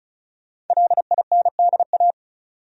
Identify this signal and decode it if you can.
Morse code, heard as a keyed tone, about 36 wpm, 700 Hz. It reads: LINDA
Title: USA Common Names - 35wpm